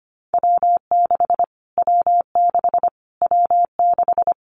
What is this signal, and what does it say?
Morse code, heard as a keyed tone, about 25 wpm, 700 Hz. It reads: W6 W6 W6
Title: Callsign Prefixes for Top 25 DXCC - Repeated 3x - 25wpm